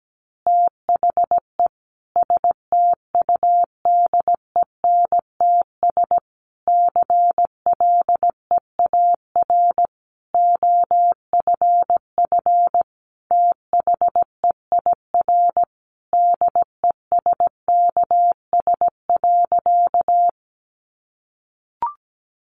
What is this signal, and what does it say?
Morse code, heard as a keyed tone, about 17 wpm, 700 Hz. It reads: THE STUDENTS CLEAR OFF THEIR DESKS.
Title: Additional Easy Phrases 17wpm